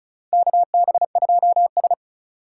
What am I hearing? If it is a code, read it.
Morse code, heard as a keyed tone, about 36 wpm, 700 Hz. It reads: KB2S
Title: K1USN SST QSO Practice - 35wpm